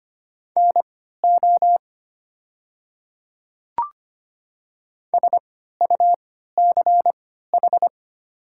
Morse code, heard as a keyed tone, about 25 wpm, spaced as for 14 wpm, 700 Hz. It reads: NO SUCH
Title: 3x Character Spacing - Top 200 Words - 25wpm